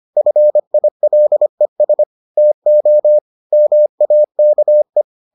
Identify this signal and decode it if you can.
Morse code, heard as a keyed tone, about 25 wpm, 600 Hz. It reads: FILES TO MAKE